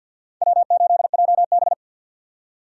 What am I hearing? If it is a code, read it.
Morse code, heard as a keyed tone, about 50 wpm, 700 Hz. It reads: W8JB
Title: USA Call Signs - 4x Character Spacing - 50wpm